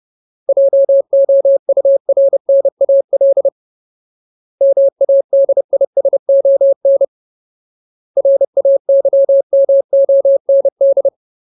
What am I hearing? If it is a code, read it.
Morse code, heard as a keyed tone, about 30 wpm, 550 Hz. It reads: JOURNAL MADISON RAYMOND
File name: Ultimate Word List / 30WPM Ultimate